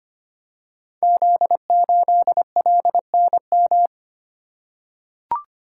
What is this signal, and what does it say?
Morse code, heard as a keyed tone, about 25 wpm, 700 Hz. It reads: Z8LNM
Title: Code Groups - 5 Letter:Numbers - 25wpm